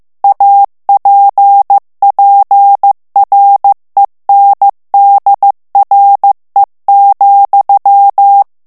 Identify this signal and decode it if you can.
Morse code, heard as a keyed tone, about 15 wpm, 800 Hz. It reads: APPRENDRE,
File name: morse mp3 / FRANCAIS TEXTE CLAIR 2 VITESSE 14 MOTS MINUTES